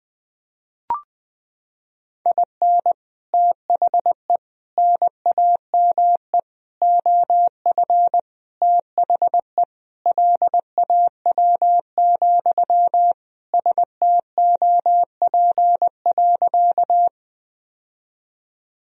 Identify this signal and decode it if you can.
Morse code, heard as a keyed tone, about 21 wpm, 700 Hz. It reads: IN THE NAME OF THE LAW, STOP.
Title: Sentences from Top 1000 Words - Review Part 3 - 20wpm